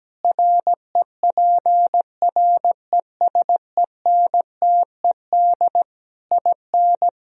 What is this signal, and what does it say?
Morse code, heard as a keyed tone, about 17 wpm, 700 Hz. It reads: REPRESENTED IN
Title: Speed-Racing - Sets of 5 Words 17wpm